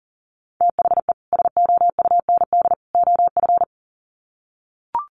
Morse code, heard as a keyed tone, about 40 wpm, 700 Hz. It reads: THE SOUND OF
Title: Sets of 3 Words 40wpm